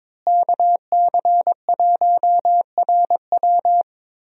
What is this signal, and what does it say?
Morse code, heard as a keyed tone, about 22 wpm, 700 Hz. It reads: KC1RW
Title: USA Call Signs - 5 Characters - Repeated 2x - 22wpm